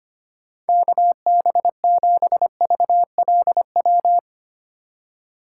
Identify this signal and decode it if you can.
Morse code, heard as a keyed tone, about 25 wpm, 700 Hz. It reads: KB7VLW
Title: USA Call Signs - 6 Characters - 4x Character Spacing - 25wpm